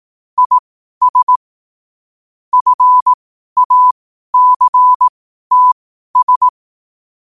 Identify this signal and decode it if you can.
Morse code, heard as a keyed tone, about 18 wpm, spaced as for 12 wpm, 1000 Hz. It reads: IS FACTS
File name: qotdfiles / 12wpm